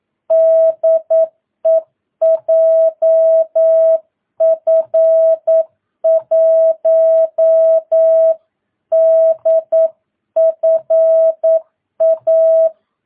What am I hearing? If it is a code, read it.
Morse code, heard as a keyed tone, about 9 wpm, 650 Hz. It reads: DE JF1DFA
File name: jh8cbh / 2022cw